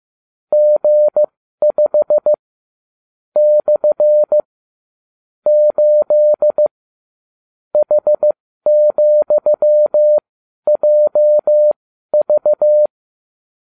Morse code, heard as a keyed tone, about 15 wpm, spaced as for 12 wpm, 600 Hz. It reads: G5 / 8 H,JV